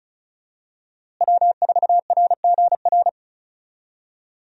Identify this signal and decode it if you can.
Morse code, heard as a keyed tone, about 36 wpm, 700 Hz. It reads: W4RGR
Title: USA Call Signs - 5 Characters - Repeated 3x - 35wpm